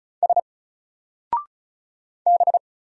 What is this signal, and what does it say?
Morse code, heard as a keyed tone, about 33 wpm, 700 Hz. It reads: S B
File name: Single-Letter-Number-35wpm